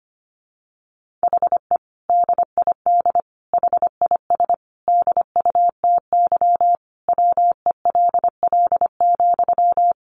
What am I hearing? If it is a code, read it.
Morse code, heard as a keyed tone, about 25 wpm, 700 Hz. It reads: HE DID HIS DUTY WELL,
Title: Sentences from Top 1500 Words 25wpm